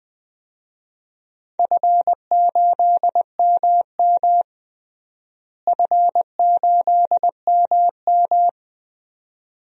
Morse code, heard as a keyed tone, about 20 wpm, 700 Hz. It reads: F8MM F8MM
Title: DXCC-100 Call Signs - 4 Characters - Repeated 2x - 20wpm